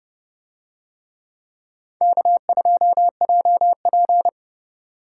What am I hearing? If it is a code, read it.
Morse code, heard as a keyed tone, about 28 wpm, 700 Hz. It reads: K2JP